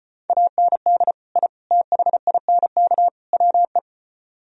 Morse code, heard as a keyed tone, about 29 wpm, 700 Hz. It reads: AND I THINK WE
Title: Speed-Racing - Sets of 4 Words 25wpm